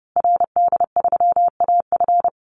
Morse code, heard as a keyed tone, about 30 wpm, 700 Hz. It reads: RD3AF
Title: Call Signs Repeated 2x - 30wpm